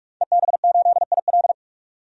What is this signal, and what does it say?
Morse code, heard as a keyed tone, about 46 wpm, 700 Hz. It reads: EB8IL
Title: DXCC-100 Call Signs - 5 Characters - 45wpm